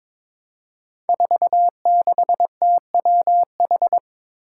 Morse code, heard as a keyed tone, about 22 wpm, 700 Hz. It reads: 46TWH
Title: Five Challenging Letter-Number Combinations 22wpm